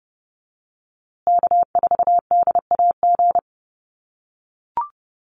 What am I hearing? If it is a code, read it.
Morse code, heard as a keyed tone, about 30 wpm, 700 Hz. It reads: K4DAG
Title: USA Call Signs Repeated 3x 30wpm